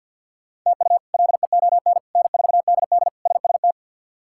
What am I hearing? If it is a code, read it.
Morse code, heard as a keyed tone, about 53 wpm, 700 Hz. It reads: TU LEON N4DD SST
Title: K1USN SST QSO Practice - 50wpm